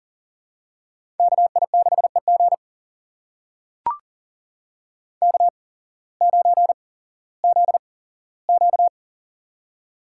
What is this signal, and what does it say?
Morse code, heard as a keyed tone, about 40 wpm, spaced as for 19 wpm, 700 Hz. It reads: KI6EG K9ZQ
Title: USA Call Signs - 8x Character Spacing - 40wpm